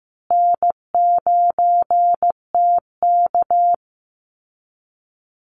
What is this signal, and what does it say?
Morse code, heard as a keyed tone, about 15 wpm, 700 Hz. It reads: N9TK
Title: Easy Call Signs Repeated 3x - 15wpm